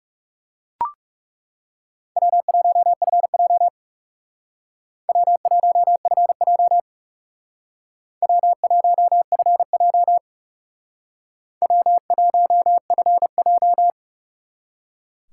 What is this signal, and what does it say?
Morse code, heard as a keyed tone, about 38 wpm, 700 Hz. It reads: W1FJ W1FJ W1FJ W1FJ